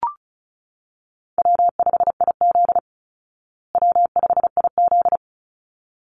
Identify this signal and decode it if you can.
Morse code, heard as a keyed tone, about 34 wpm, 700 Hz. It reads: W5IZ W5IZ
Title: Call Signs Repeated 3x - 35wpm